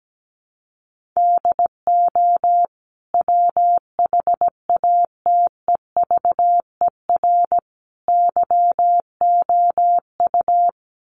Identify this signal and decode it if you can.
Morse code, heard as a keyed tone, about 17 wpm, 700 Hz. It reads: DO WHATEVER YOU
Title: Sentences from Top 1100 Words 17wpm